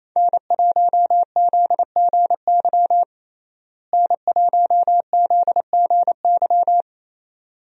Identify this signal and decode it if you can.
Morse code, heard as a keyed tone, about 28 wpm, 700 Hz. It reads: N1ZGY N1ZGY
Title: USA Call Signs - 5 Characters - Repeated 2x - 28wpm